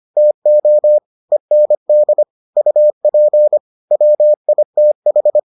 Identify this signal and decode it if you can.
Morse code, heard as a keyed tone, about 25 wpm, 600 Hz. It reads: TO END UP WITH